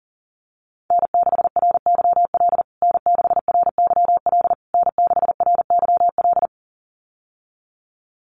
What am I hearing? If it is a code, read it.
Morse code, heard as a keyed tone, about 40 wpm, 700 Hz. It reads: N6RYL N6RYL N6RYL
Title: USA Call Signs Repeated 3x 40wpm